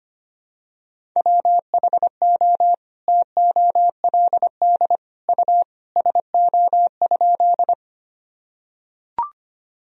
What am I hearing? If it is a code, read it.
Morse code, heard as a keyed tone, about 25 wpm, 700 Hz. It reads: WHO TOLD U SO?
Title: Sentences from Top 400 Words and Abbreviations 25wpm